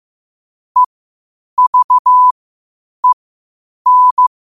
Morse code, heard as a keyed tone, about 15 wpm, spaced as for 8 wpm, 1000 Hz. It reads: EVEN
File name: qotdfiles / 8wpm